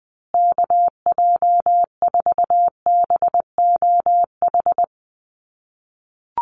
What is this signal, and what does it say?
Morse code, heard as a keyed tone, about 20 wpm, 700 Hz. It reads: KJ4BOH
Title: USA Call Signs Repeated 2x 20wpm